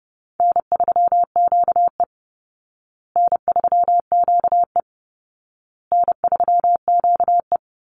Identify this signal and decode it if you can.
Morse code, heard as a keyed tone, about 30 wpm, 700 Hz. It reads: N3QE N3QE N3QE
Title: Easy Call Signs Repeated 3x - 30wpm